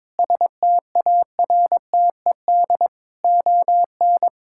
Morse code, heard as a keyed tone, about 22 wpm, 700 Hz. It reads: STARTED ON
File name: practice / 22wpm / Sets of 7 Words - 22wpm